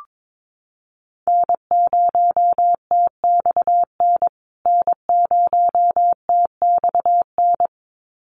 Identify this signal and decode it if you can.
Morse code, heard as a keyed tone, about 22 wpm, 700 Hz. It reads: N0TXN N0TXN
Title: USA Call Signs Repeated 2x 22wpm